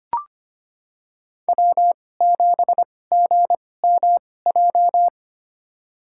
Morse code, heard as a keyed tone, about 25 wpm, spaced as for 18 wpm, 700 Hz. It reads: W7GMJ
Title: USA Call Signs - 5 Characters - 2x Character Spacing - 25wpm